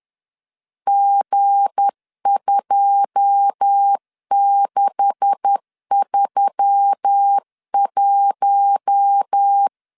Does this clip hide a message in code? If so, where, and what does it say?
Yes: Morse code from 0.9 to 9.7 s, heard as a keyed tone, about 10 wpm, 800 Hz. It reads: G2631